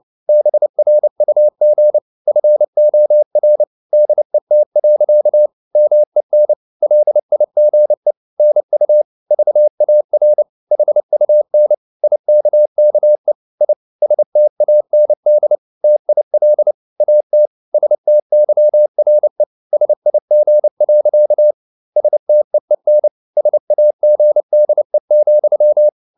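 Morse code, heard as a keyed tone, about 29 wpm, 600 Hz. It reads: BRUG FOR DET. MEN LIGE NU VAR HUN IKKE I STAND TIL AT STYRE SIG. STEEN SAGDE,
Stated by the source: Fast (QRQ) Morse at 29 wpm, from CW med Gnister.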